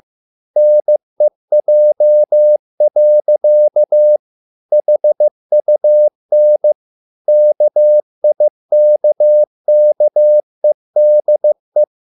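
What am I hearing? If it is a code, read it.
Morse code, heard as a keyed tone, about 15 wpm, 600 Hz. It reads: NEJ. HUN KIKKEDE
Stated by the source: Morse code at a slow 15 wpm from CW med Gnister.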